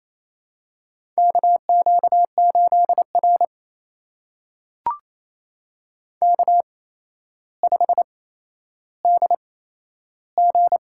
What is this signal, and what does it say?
Morse code, heard as a keyed tone, about 28 wpm, spaced as for 12 wpm, 700 Hz. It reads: KQ8R K5DG
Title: USA Call Signs - 4 Characters - 8x Character Spacing - 28wpm